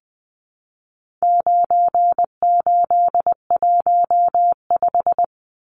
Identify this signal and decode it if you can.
Morse code, heard as a keyed tone, about 20 wpm, 700 Hz. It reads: 9815
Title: Numbers - 4-Digit 20wpm